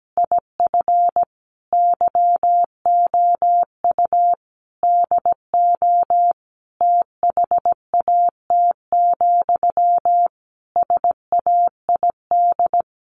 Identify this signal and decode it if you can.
Morse code, heard as a keyed tone, about 17 wpm, 700 Hz. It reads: IF YOU DO THAT, SAID
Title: Sentences from Top 1500 Words 17wpm